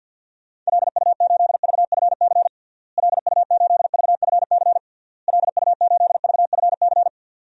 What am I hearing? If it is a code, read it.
Morse code, heard as a keyed tone, about 50 wpm, 700 Hz. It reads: LU8VFC LU8VFC LU8VFC
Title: DXCC-100 Call Signs - 6 Characters - Repeated 3x - 50wpm